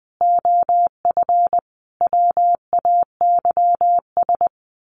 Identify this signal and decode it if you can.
Morse code, heard as a keyed tone, about 20 wpm, 700 Hz. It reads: OF WAYS
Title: Sets of 5 Words Encore 20wpm